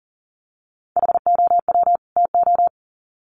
Morse code, heard as a keyed tone, about 40 wpm, 700 Hz. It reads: HOW TO
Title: Sets of 2 Words 40wpm